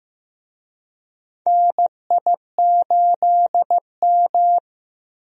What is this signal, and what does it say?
Morse code, heard as a keyed tone, about 15 wpm, 700 Hz. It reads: NI8M
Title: USA Call Signs - 4 Characters - 15wpm